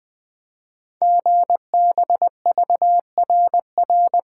Morse code, heard as a keyed tone, about 20 wpm, 700 Hz. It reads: GBVRR
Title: Code Groups - 5 Letters - 20wpm